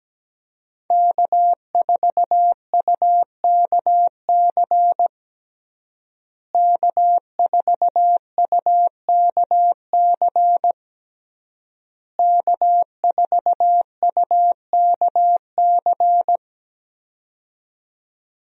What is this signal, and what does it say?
Morse code, heard as a keyed tone, about 17 wpm, 700 Hz. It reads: K4UKC K4UKC K4UKC
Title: USA Call Signs - 5 Characters - Repeated 3x - 17wpm